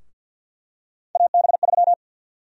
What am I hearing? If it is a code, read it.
Morse code, heard as a keyed tone, about 51 wpm, 700 Hz. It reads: AB3
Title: USA Call Sign Prefixes - Repeated 2x - 50wpm